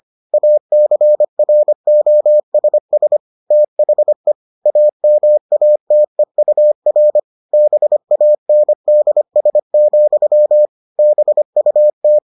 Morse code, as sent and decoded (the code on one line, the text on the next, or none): .- -.-. .-. --- ... ... / - .... / .- -- .- - . ..- .-. / -... .- -. -.. ... --..-- / -... ..- -
ACROSS THE AMATEUR BANDS, BUT